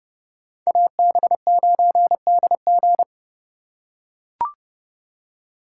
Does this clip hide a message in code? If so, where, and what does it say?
Yes: Morse code from 0.7 to 3.0 s, heard as a keyed tone, about 29 wpm, 700 Hz. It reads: AB9DG